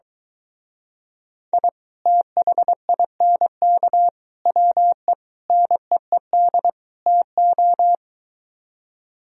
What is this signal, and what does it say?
Morse code, heard as a keyed tone, about 23 wpm, 700 Hz. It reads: I THINK WE NEED TO